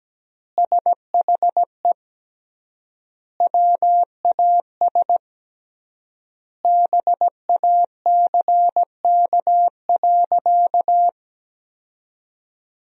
Morse code, heard as a keyed tone, about 17 wpm, 700 Hz. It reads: SHE WAS BACK.
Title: 3x Word-Spacing - Sentences from Top 500 Words - 17wpm